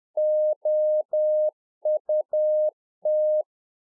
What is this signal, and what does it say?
Morse code, heard as a keyed tone, about 10 wpm, 610 Hz. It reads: OUT